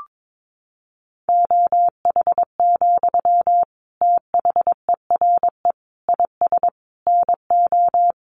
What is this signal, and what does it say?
Morse code, heard as a keyed tone, about 22 wpm, 700 Hz. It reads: OH, THERE IS NO
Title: Sentences from Top 1500 Words 22wpm